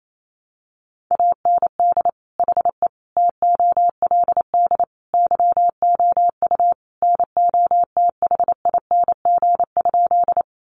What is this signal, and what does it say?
Morse code, heard as a keyed tone, about 28 wpm, 700 Hz. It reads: AND HE TOLD YOU NOTHING?
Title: Sentences from Top 500 Words 28wpm